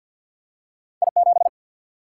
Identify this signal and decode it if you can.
Morse code, heard as a keyed tone, about 53 wpm, 700 Hz. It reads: I 7
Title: Single-Letter Single-Number 50wpm